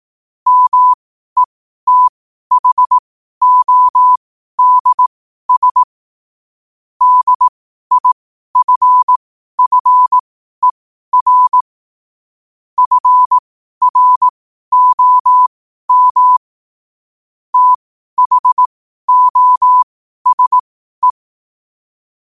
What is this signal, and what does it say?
Morse code, heard as a keyed tone, about 18 wpm, spaced as for 12 wpm, 1000 Hz. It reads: METHODS DIFFER FROM THOSE
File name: qotdfiles / 12wpm